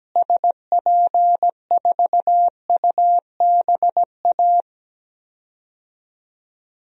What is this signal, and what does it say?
Morse code, heard as a keyed tone, about 17 wpm, 700 Hz. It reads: SP4UBA
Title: DXCC-100 Call Signs - 6 Characters - Repeated 2x - 17wpm